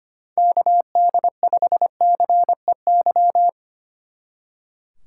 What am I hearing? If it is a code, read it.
Morse code, heard as a keyed tone, about 27 wpm, 700 Hz. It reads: KD5CEY
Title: Speed Racing - USA 2x3 Call Signs 25wpm